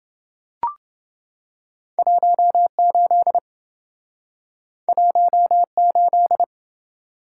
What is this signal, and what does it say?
Morse code, heard as a keyed tone, about 27 wpm, 700 Hz. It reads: 18 18